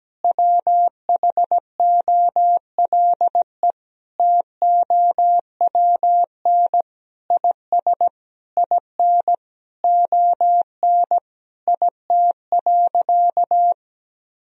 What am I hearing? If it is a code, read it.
Morse code, heard as a keyed tone, about 17 wpm, 700 Hz. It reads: WHOLE TOWN IS IN ON IT.
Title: Sentences from Top 1000 Words - Review Part 4 - 17wpm